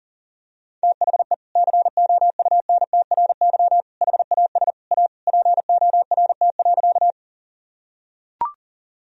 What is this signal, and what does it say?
Morse code, heard as a keyed tone, about 40 wpm, 700 Hz. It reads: THE COUNTRY HAS A PORT.
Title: Sentences from Top 1000 Words - Review Part 1 - 40wpm